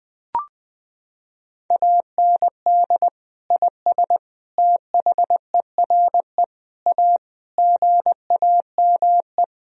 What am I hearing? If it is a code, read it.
Morse code, heard as a keyed tone, about 20 wpm, 700 Hz. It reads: AND IS THERE A GAME
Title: Sentences from Top 1000 Words - Review Part 3 - 20wpm